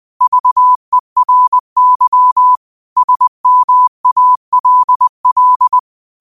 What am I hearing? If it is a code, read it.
Morse code, heard as a keyed tone, about 20 wpm, 1000 Hz. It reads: VERY SMALL
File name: qotdfiles / 20wpm